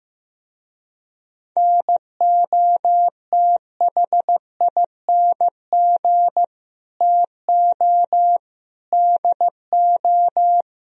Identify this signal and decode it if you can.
Morse code, heard as a keyed tone, about 15 wpm, 700 Hz. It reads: NOTHING TO DO
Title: Speed-Racing - Sets of 4 Words 15wpm